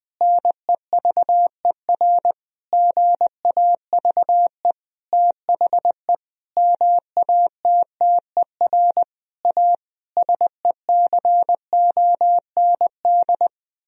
Morse code, heard as a keyed tone, about 20 wpm, 700 Hz. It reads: NEVER GAVE THE MATTER A SECOND
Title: Sentences from Top 1000 Words - Review Part 2 - 20wpm